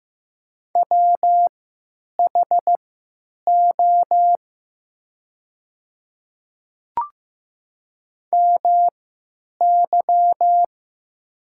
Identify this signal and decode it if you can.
Morse code, heard as a keyed tone, about 15 wpm, spaced as for 9 wpm, 700 Hz. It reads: WHO MY
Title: 3x Character Spacing - Top 100 Words - 15wpm